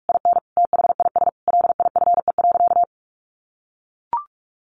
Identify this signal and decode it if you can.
Morse code, heard as a keyed tone, about 44 wpm, 700 Hz. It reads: IN THIS LIFE.
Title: Sentences from Top 1200 Words 45wpm